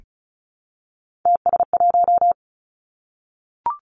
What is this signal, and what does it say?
Morse code, heard as a keyed tone, about 36 wpm, 700 Hz. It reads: TS1
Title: Three Challenging Letter-Number Combinations 35wpm